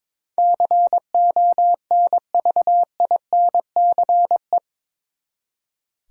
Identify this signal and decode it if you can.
Morse code, heard as a keyed tone, about 21 wpm, 700 Hz. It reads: CONVINCE
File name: practice / 22wpm / Eight Letter Words - 22wpm